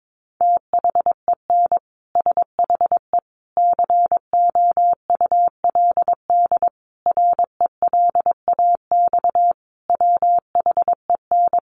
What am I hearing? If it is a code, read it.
Morse code, heard as a keyed tone, about 22 wpm, 700 Hz. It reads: THEN SHE COULD RELAX WHEN
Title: Sentences from Top 1400 Words 22wpm